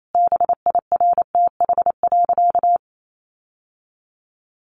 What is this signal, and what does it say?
Morse code, heard as a keyed tone, about 28 wpm, 700 Hz. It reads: BIRTH.
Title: Sentences from Top 800 Words 28wpm